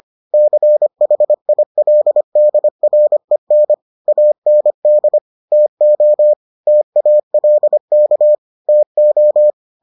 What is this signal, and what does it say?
Morse code, heard as a keyed tone, about 25 wpm, 600 Hz. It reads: CHILDREN AND TO TALK TO